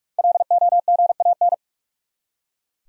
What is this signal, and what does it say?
Morse code, heard as a keyed tone, about 43 wpm, 700 Hz. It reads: LOGAN
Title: USA Common Names - 45wpm